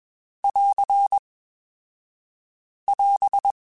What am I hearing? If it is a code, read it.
Morse code, heard as a keyed tone, about 22 wpm, 2550 Hz. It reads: <AR> <AS>
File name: Quotes Spaced / Quotes Spaced-mp3s / Character sp @20